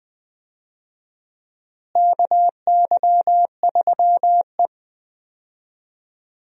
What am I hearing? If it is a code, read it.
Morse code, heard as a keyed tone, about 20 wpm, 700 Hz. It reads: KY3E
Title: USA Call Signs - 4 Characters - Repeated 2x - 20wpm